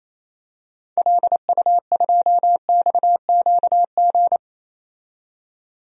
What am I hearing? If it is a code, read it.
Morse code, heard as a keyed tone, about 28 wpm, 700 Hz. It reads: LU2XQG
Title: DXCC-100 Call Signs - 6 Characters - 28wpm